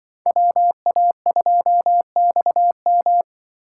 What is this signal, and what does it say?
Morse code, heard as a keyed tone, about 25 wpm, 700 Hz. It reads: WA2XM
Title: Speed Racing - USA 2x2 Call Signs 20wpm